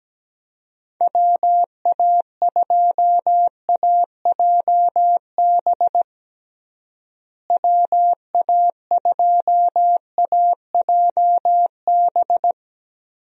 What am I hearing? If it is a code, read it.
Morse code, heard as a keyed tone, about 17 wpm, 700 Hz. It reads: WA2AJB WA2AJB
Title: USA Call Signs - 6 Characters - Repeated 3x - 17wpm